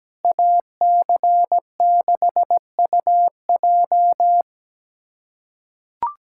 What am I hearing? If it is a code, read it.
Morse code, heard as a keyed tone, about 17 wpm, 700 Hz. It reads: AC6UJ
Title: USA Callsign DE Callsign - 5 Characters - 17wpm